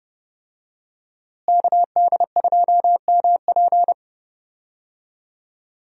Morse code, heard as a keyed tone, about 31 wpm, 700 Hz. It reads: KD2MP
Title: USA Call Signs - 5 Characters - Repeated 2x - 30wpm